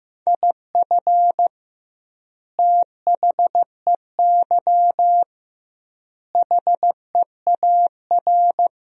Morse code, heard as a keyed tone, about 15 wpm, 700 Hz. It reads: IF THEY HEAR
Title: 2x Word-Spacing - Sentences from Top 400 Words - 15wpm